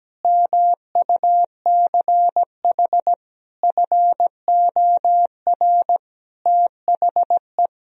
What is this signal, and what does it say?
Morse code, heard as a keyed tone, about 17 wpm, 700 Hz. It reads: MUCH FOR THE
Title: Sentences from Top 1000 Words - Review Part 3 - 17wpm